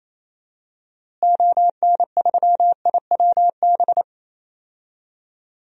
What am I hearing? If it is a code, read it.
Morse code, heard as a keyed tone, about 28 wpm, 700 Hz. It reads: ON3IWB
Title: DXCC-100 Call Signs - 6 Characters - Repeated 3x - 28wpm